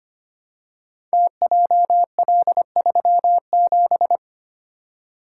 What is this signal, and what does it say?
Morse code, heard as a keyed tone, about 25 wpm, 700 Hz. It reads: TJL37
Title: Code Groups - 5 Letter:Numbers - 25wpm